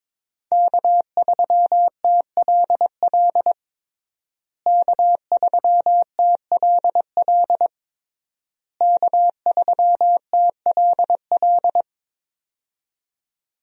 Morse code, heard as a keyed tone, about 22 wpm, 700 Hz. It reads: K3TLL K3TLL K3TLL
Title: USA Call Signs - 5 Characters - Repeated 3x - 22wpm